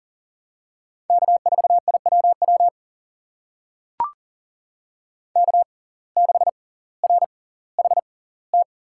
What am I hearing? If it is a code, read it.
Morse code, heard as a keyed tone, about 39 wpm, spaced as for 20 wpm, 700 Hz. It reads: K4IWW K6RHT
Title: USA Call Signs - 5 Characters - 6x Character Spacing - 40wpm